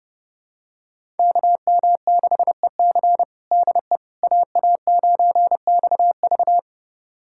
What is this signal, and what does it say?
Morse code, heard as a keyed tone, about 30 wpm, 700 Hz. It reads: KM6EC DE AA9XV
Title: USA Callsign DE Callsign - 5 Characters - 30wpm